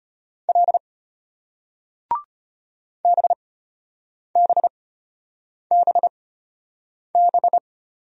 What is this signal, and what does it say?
Morse code, heard as a keyed tone, about 35 wpm, spaced as for 8 wpm, 700 Hz. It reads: L BBBB